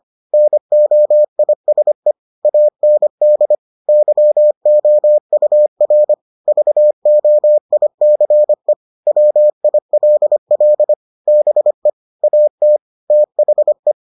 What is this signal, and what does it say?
Morse code, heard as a keyed tone, about 25 wpm, 600 Hz. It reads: NOISE AND YOUR VOICE WILL BE AT THE